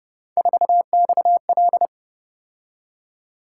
Morse code, heard as a keyed tone, about 31 wpm, 700 Hz. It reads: 4XL
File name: Speed-Racing - Three Letter-Number Combinations - 25wpm